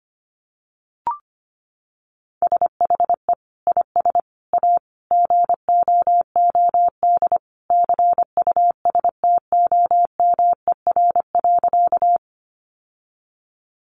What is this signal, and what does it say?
Morse code, heard as a keyed tone, about 25 wpm, 700 Hz. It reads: SHE IS A GOOD CUSTOMER.
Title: Sentences from Top 1100 Words 25wpm